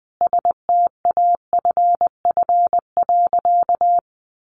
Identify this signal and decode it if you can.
Morse code, heard as a keyed tone, about 20 wpm, 700 Hz. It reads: STAFF.
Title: Sentences from Top 1200 Words 20wpm